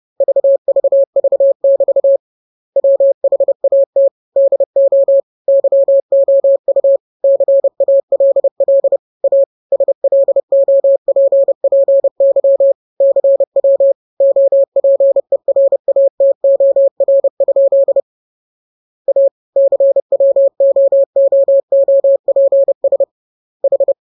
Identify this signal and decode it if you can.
Morse code, heard as a keyed tone, about 30 wpm, 550 Hz. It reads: VVV<BT> WHAT DO YOU CALL A SLOPPY CW OPERATOR? A CWOOOPS H
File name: jokes / joke 24 30 0000